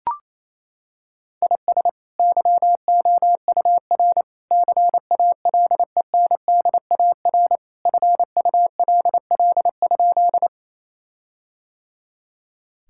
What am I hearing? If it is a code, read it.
Morse code, heard as a keyed tone, about 28 wpm, 700 Hz. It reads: IS YOUR CALENDAR FULL?
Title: Sentences from Top 1600 Words 28wpm